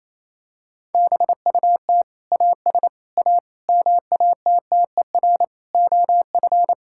Morse code, heard as a keyed tone, about 28 wpm, 700 Hz. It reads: BUT AS A MATTER OF
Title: Sentences from Top 1000 Words - Review Part 6 - 28wpm